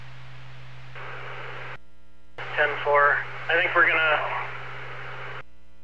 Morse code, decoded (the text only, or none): none